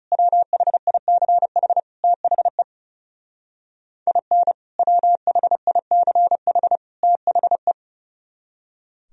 Morse code, heard as a keyed tone, about 36 wpm, 700 Hz. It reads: WHICH THE IN WHICH THE
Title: Speed-Racing - Sets of 3 Words 30wpm